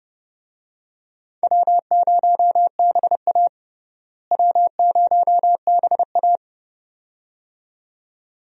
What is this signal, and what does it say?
Morse code, heard as a keyed tone, about 30 wpm, 700 Hz. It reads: W0BA W0BA
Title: USA Call Signs - 4 Characters - Repeated 2x - 30wpm